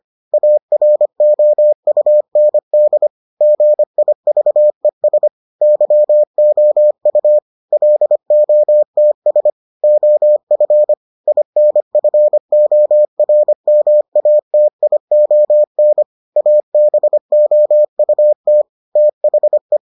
Morse code, heard as a keyed tone, about 25 wpm, 600 Hz. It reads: AROUND GIVES YOU LOTS OF INFORMATION ABOUT THE